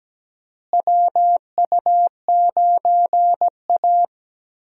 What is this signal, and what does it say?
Morse code, heard as a keyed tone, about 17 wpm, 700 Hz. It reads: WU9A
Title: USA Call Signs - 4 Characters - Repeated 2x - 17wpm